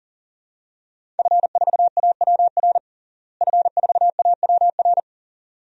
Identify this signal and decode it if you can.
Morse code, heard as a keyed tone, about 37 wpm, 700 Hz. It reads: F4AWR F4AWR
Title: DXCC-100 Call Signs - 5 Characters - Repeated 2x - 40wpm